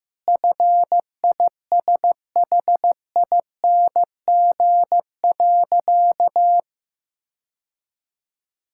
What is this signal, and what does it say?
Morse code, heard as a keyed tone, about 16 wpm, 700 Hz. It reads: FISHING.
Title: Sentences from Top 1000 Words - Review Part 5 - 15wpm